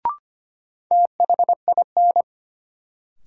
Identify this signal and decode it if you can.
Morse code, heard as a keyed tone, about 25 wpm, 700 Hz. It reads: THIN
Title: Sentences from Top 1200 Words - New Word Drill Alt - 25wpm